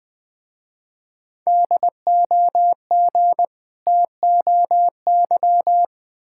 Morse code, heard as a keyed tone, about 20 wpm, 690 Hz. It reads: DOG TOY
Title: Difficult Words with Hints - 20wpm